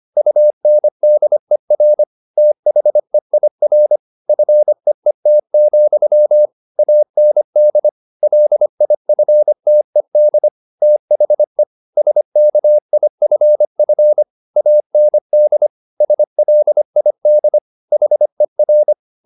Morse code, heard as a keyed tone, about 25 wpm, 600 Hz. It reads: UNDER THEIR FEET, AND LIFTED THE SKIFF AND SLID HER
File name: the old man and the sea part 1 25wpm